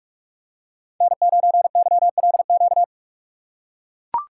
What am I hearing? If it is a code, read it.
Morse code, heard as a keyed tone, about 44 wpm, 700 Hz. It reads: N9YLQ
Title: USA Call Signs - 5 Characters - 3x Character Spacing - 45wpm